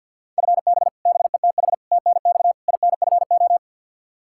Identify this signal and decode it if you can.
Morse code, heard as a keyed tone, about 50 wpm, 700 Hz. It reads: FB BETH TNX INFO
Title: Practice QSO - Exchange 3 - More Info - 50wpm